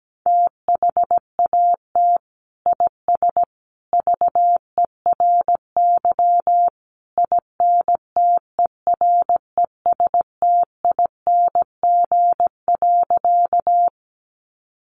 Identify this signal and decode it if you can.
Morse code, heard as a keyed tone, about 17 wpm, 700 Hz. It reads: THAT IS VERY INTERESTING.
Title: Sentences from Top 1000 Words 17wpm